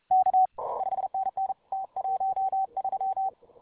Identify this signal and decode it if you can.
Morse code, heard as a keyed tone, about 31 wpm, 750 Hz. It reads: K 5NN T13